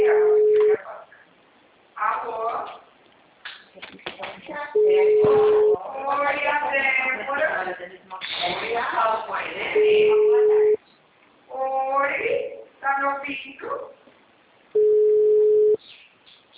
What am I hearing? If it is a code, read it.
Ringback tone (Europe, 425 Hz), with speech.